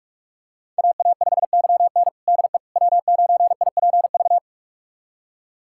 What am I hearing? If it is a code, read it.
Morse code, heard as a keyed tone, about 44 wpm, 700 Hz. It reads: AA5YN DE W9IPV
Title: USA Callsign DE Callsign - 5 Characters - 45wpm